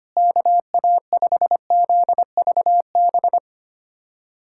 Morse code, heard as a keyed tone, about 25 wpm, 700 Hz. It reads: KA5ZVB
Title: USA Callsign DE Callsign 25wpm